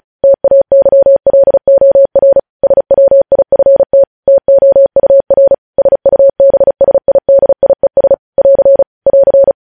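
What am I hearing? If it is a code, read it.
Morse code, heard as a keyed tone, about 36 wpm, 550 Hz. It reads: TAYLOR SWIFT TOUR SUBSIDIES <AR> <AR>
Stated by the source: Morse practice at 35 wpm sending news headlines.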